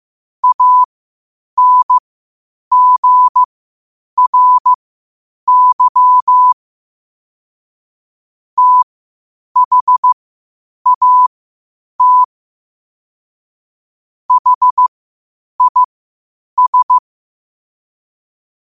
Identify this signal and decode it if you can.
Morse code, heard as a keyed tone, about 15 wpm, spaced as for 8 wpm, 1000 Hz. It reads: ANGRY THAT HIS
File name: qotdfiles / 8wpm